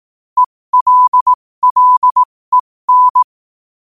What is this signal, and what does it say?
Morse code, heard as a keyed tone, about 18 wpm, spaced as for 15 wpm, 1000 Hz. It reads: ELLEN
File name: qotdfiles / 15wpm